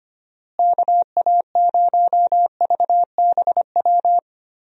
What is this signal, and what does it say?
Morse code, heard as a keyed tone, about 25 wpm, 700 Hz. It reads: KA0VBW
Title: USA Call Signs - 6 Characters - Repeated 2x - 25wpm